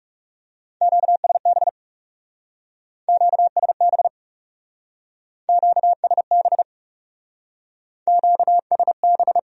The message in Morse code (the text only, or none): QSB QSB QSB QSB